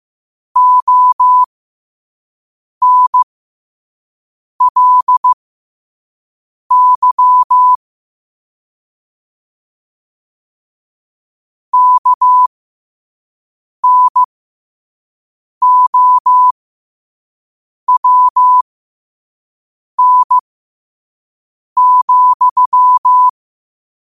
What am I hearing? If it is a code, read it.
Morse code, heard as a keyed tone, about 15 wpm, spaced as for 5 wpm, 1000 Hz. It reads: ONLY KNOWN,
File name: qotdfiles / 5wpm